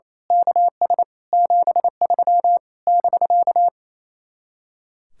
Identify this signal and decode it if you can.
Morse code, heard as a keyed tone, about 26 wpm, 700 Hz. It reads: KS 73 <BK>
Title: Speed-Racing Common QSO Phrases Encore - 28wpm